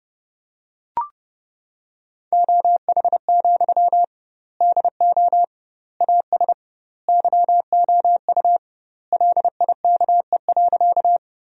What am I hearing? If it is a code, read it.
Morse code, heard as a keyed tone, about 30 wpm, 700 Hz. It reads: OH, DO AS YOU LIKE.
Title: 2x Word-Spacing - Sentences from Top 500 Words - 30wpm